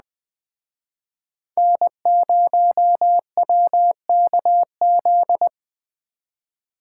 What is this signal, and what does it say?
Morse code, heard as a keyed tone, about 20 wpm, 700 Hz. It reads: N0WKZ
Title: Speed Racing - USA 1x3 Call Signs 17wpm